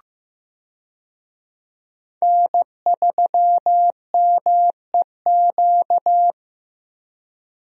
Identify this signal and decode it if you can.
Morse code, heard as a keyed tone, about 15 wpm, 700 Hz. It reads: N3MEQ
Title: USA Call Signs - 5 Characters - Repeated 2x - 15wpm